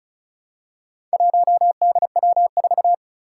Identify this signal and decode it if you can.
Morse code, heard as a keyed tone, about 36 wpm, 700 Hz. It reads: 1DW4
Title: Four Challenging Letter-Number Combinations 35wpm